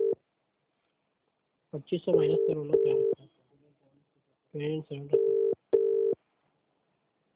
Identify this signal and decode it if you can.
Ringback tone (UK), 400 + 450 Hz, with speech.